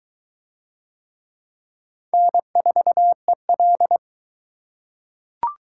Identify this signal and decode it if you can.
Morse code, heard as a keyed tone, about 22 wpm, 700 Hz. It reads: N4EL